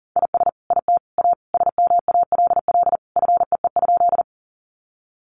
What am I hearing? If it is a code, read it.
Morse code, heard as a keyed tone, about 40 wpm, 700 Hz. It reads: IS IT A SMALL FEE?
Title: Sentences from Top 1500 Words 40wpm